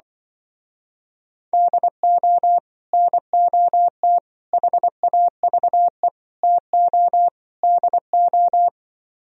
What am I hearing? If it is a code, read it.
Morse code, heard as a keyed tone, about 24 wpm, 700 Hz. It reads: DO NOT HAVE TO DO
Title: Speed-Racing - Sets of 5 Words 20wpm